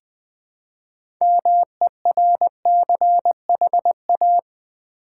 Morse code, heard as a keyed tone, about 20 wpm, 700 Hz. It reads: MERCHA
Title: Build A Word - 2 Letters At A Time - 8-Letter Words - 20wpm